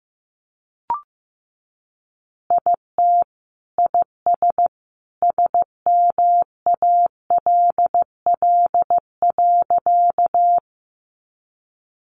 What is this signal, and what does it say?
Morse code, heard as a keyed tone, about 15 wpm, 700 Hz. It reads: IT IS SMALL.
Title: Easy Phrases 15wpm